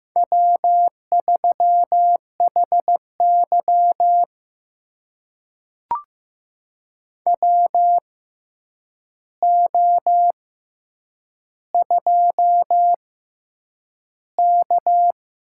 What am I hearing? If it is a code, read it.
Morse code, heard as a keyed tone, about 15 wpm, spaced as for 8 wpm, 700 Hz. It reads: W3HY WO2K
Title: USA Call Signs - 4 Characters - 6x Character Spacing - 15wpm